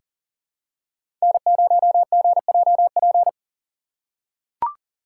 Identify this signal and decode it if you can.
Morse code, heard as a keyed tone, about 39 wpm, 700 Hz. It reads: N0GJP
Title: USA Call Signs - 5 Characters - 3x Character Spacing - 40wpm